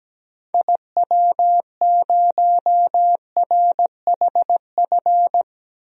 Morse code, heard as a keyed tone, about 17 wpm, 700 Hz. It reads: IW0RHF
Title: DXCC-100 Call Signs - 6 Characters - Repeated 3x - 17wpm